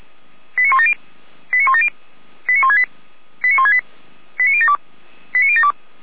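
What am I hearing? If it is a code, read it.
Five-tone selective call (ZVEI), heard as quick runs of five tones, tones of about 70 ms. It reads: 89170 89170 89168 89168 89072 89072
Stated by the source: ZVEI five-tone selective call 89560, received on SDR.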